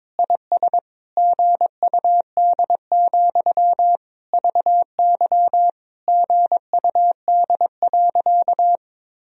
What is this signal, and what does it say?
Morse code, heard as a keyed tone, about 22 wpm, 700 Hz. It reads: IS GUD, VY GUD.
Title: Sentences from Top 200 Words and Abbreviations 22wpm